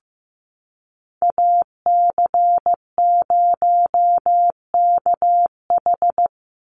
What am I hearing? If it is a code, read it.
Morse code, heard as a keyed tone, about 15 wpm, 700 Hz. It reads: AC0KH
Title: USA Call Signs 15wpm